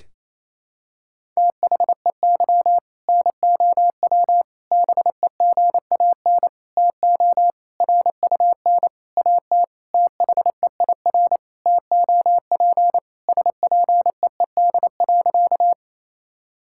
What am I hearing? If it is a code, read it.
Morse code, heard as a keyed tone, about 28 wpm, 700 Hz. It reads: THEY NOW BEGAN TO RUN AT THEIR TOP SPEED.
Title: Sentences from Top 1000 Words - Review Part 5 - 28wpm